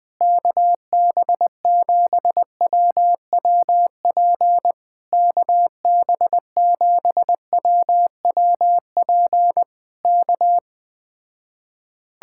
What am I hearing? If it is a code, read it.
Morse code, heard as a keyed tone, about 20 wpm, 700 Hz. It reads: KB7WWP KB7WWP K